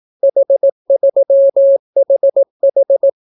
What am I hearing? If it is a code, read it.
Morse code, heard as a keyed tone, about 18 wpm, 550 Hz. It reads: H3HH